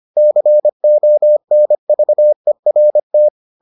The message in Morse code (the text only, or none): CONVERT